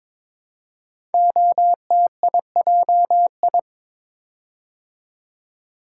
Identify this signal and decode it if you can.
Morse code, heard as a keyed tone, about 22 wpm, 700 Hz. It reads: OTIJI
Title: Code Groups - 5 Letters - 22wpm